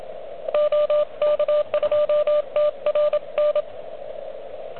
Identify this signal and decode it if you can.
Morse code, heard as a keyed tone, about 25 wpm, 600 Hz, with radio noise. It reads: OK2TRN
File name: OK QRP 2022 / Zvuk / REC31